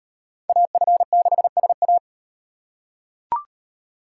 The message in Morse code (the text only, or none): AF6SA